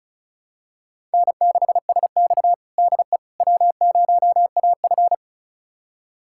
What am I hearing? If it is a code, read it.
Morse code, heard as a keyed tone, about 34 wpm, 700 Hz. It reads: N6SX DE W0AF
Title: USA Callsign DE Callsign - 4 Characters - 35wpm